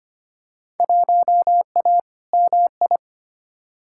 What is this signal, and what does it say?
Morse code, heard as a keyed tone, about 25 wpm, 700 Hz. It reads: 1A MI
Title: Field Day Exchanges - 25wpm